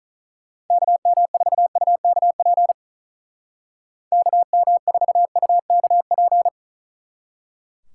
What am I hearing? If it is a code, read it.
Morse code, heard as a keyed tone, about 45 wpm, 700 Hz. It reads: KM4UKP KM4UKP